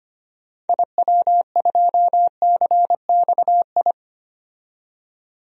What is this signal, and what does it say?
Morse code, heard as a keyed tone, about 25 wpm, 700 Hz. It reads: IW2CXI
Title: DXCC-100 Call Signs - 6 Characters - 25wpm